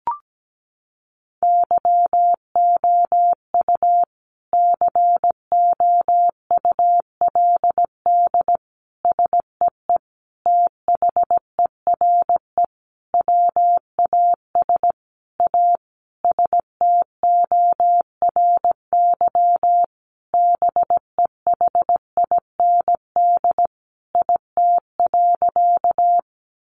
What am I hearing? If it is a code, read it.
Morse code, heard as a keyed tone, about 17 wpm, 700 Hz. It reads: YOU COULD SEE THERE WAS A STORY BEHIND IT.
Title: Sentences from Top 500 Words 17wpm